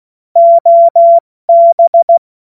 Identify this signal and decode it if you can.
Morse code, heard as a keyed tone, about 16 wpm, 680 Hz. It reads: OB